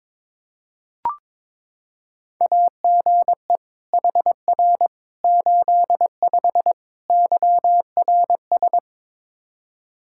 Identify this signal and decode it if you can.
Morse code, heard as a keyed tone, about 22 wpm, 700 Hz. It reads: AGE HR 85 YRS
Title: Practice QSO - Exchange 3 - More Info - 22wpm